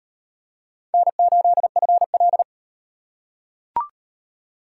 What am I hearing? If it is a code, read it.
Morse code, heard as a keyed tone, about 36 wpm, 700 Hz. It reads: N8FL